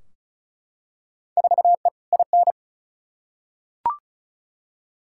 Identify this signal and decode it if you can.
Morse code, heard as a keyed tone, about 36 wpm, 700 Hz. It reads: 4E IN
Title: Field Day Exchanges - 35wpm